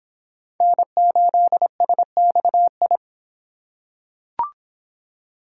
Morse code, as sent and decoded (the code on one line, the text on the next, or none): -. ---.. ... -..- ..
N8SXI